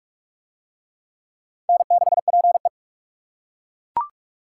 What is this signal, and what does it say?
Morse code, heard as a keyed tone, about 40 wpm, 700 Hz. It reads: N6PE